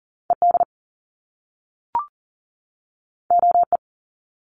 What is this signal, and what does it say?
Morse code, heard as a keyed tone, about 40 wpm, 700 Hz. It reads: ED OE
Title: Top Two Letter Combinations in Words 40wpm